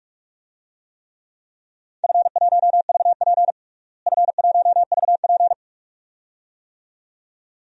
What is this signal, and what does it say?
Morse code, heard as a keyed tone, about 46 wpm, 700 Hz. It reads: F1VP F1VP
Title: DXCC-100 Call Signs - 4 Characters - Repeated 2x - 45wpm